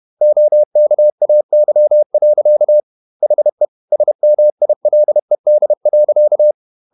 Morse code, heard as a keyed tone, about 31 wpm, 600 Hz. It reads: OKAY. HE SMILED.
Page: Never 31wpm | CW med Gnister